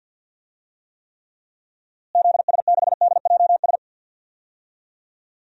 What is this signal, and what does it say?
Morse code, heard as a keyed tone, about 47 wpm, 700 Hz. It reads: ZS6DJS
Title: DXCC-100 Call Signs - 6 Characters - Repeated 2x - 50wpm